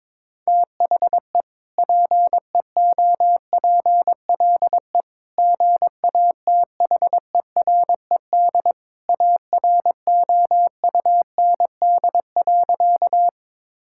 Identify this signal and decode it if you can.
Morse code, heard as a keyed tone, about 22 wpm, 700 Hz. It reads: THE PEOPLE GATHERED AROUND.
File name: practice / 22wpm / Sentences with 2-Word Phrases - 22wpm